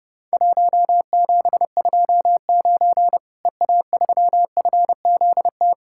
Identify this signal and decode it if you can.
Morse code, heard as a keyed tone, about 30 wpm, 700 Hz. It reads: 1729 EA3FZT
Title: Worked All Europe DX Contest - QTC Exchange - 30wpm